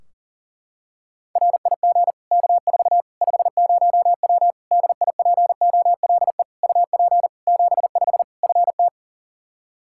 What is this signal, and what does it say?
Morse code, heard as a keyed tone, about 40 wpm, 700 Hz. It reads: RIG K4 50W DIPOLE UP 75 FT
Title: Practice QSO - Exchange 2 - Info - 40wpm